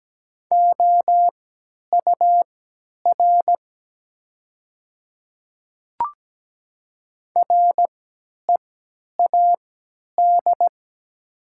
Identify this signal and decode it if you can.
Morse code, heard as a keyed tone, about 17 wpm, spaced as for 10 wpm, 700 Hz. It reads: OUR READ
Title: 3x Character Spacing - Top 200 Words - 17wpm